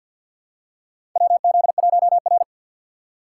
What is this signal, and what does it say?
Morse code, heard as a keyed tone, about 51 wpm, 700 Hz. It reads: WZ1R
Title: USA Call Signs - 4 Characters - 3x Character Spacing - 50wpm